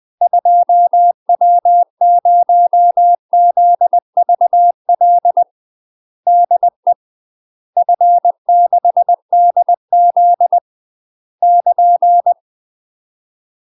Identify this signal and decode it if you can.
Morse code, heard as a keyed tone, about 20 wpm, 700 Hz. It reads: 2W0ZVL DE F6DZ <KN>
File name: UK MASTER QSOs TRAINING / 20WPM / UK MASTER QSOs 0003